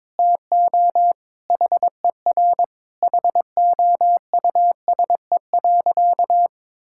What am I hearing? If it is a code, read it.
Morse code, heard as a keyed tone, about 22 wpm, 700 Hz. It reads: TO HER HOUSE.
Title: Sentences from Top 1000 Words - Review Part 6 - 22wpm